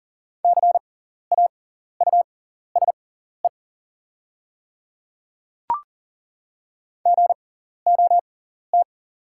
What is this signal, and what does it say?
Morse code, heard as a keyed tone, about 37 wpm, spaced as for 14 wpm, 690 Hz. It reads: CAUSE GOT
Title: 6x Character Spacing - Top 300 Words - 40wpm